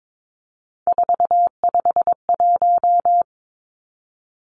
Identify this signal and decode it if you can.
Morse code, heard as a keyed tone, about 22 wpm, 700 Hz. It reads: 451
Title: Three Challenging Letter-Number Combinations 22wpm